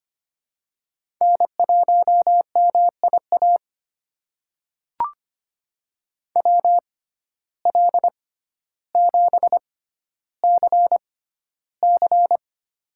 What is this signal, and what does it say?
Morse code, heard as a keyed tone, about 25 wpm, spaced as for 13 wpm, 700 Hz. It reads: N1MIA WL7CC
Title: USA Call Signs - 5 Characters - 6x Character Spacing - 25wpm